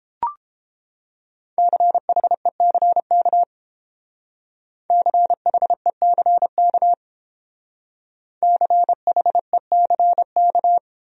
Morse code, heard as a keyed tone, about 31 wpm, 700 Hz. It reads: CHECK CHECK CHECK